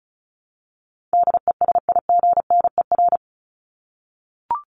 Morse code, heard as a keyed tone, about 34 wpm, 700 Hz. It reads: DESIGNER
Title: Top 1501-1750 Words 35wpm